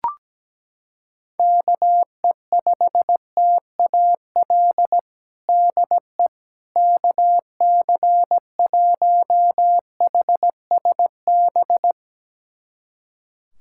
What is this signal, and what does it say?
Morse code, heard as a keyed tone, about 17 wpm, 700 Hz. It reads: KE5TAL DE KC1HSB
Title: USA Callsign DE Callsign - 6 Characters - 17wpm